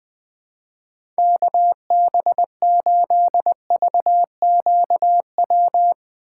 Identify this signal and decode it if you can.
Morse code, heard as a keyed tone, about 20 wpm, 700 Hz. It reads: KB8VQW
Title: USA Call Signs - 4x Character Spacing - 20wpm